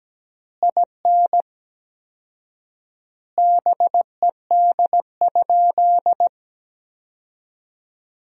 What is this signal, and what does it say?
Morse code, heard as a keyed tone, about 17 wpm, 700 Hz. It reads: IN BED?
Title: 4x Word-Spacing - Sentences from Top 500 Words - 17wpm